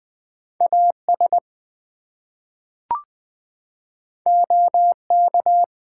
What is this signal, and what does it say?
Morse code, heard as a keyed tone, about 20 wpm, 700 Hz. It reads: AS OK
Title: Two Letter Words - 20wpm